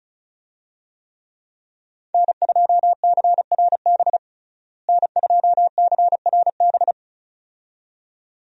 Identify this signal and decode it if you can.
Morse code, heard as a keyed tone, about 36 wpm, 700 Hz. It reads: N2CRB N2CRB
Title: USA Call Signs - 5 Characters - Repeated 2x - 35wpm